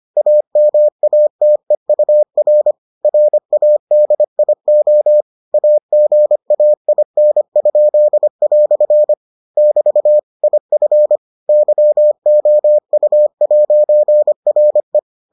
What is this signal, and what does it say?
Morse code, heard as a keyed tone, about 25 wpm, 600 Hz. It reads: AMATEUR RADIO AGAIN?" <BT> IF YOU'RE